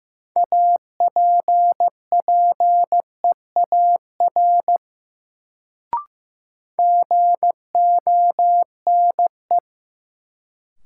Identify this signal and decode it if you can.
Morse code, heard as a keyed tone, about 15 wpm, 700 Hz. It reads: APPEAR GONE
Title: Sentences from Top 1200 Words - New Word Drill Alt - 15wpm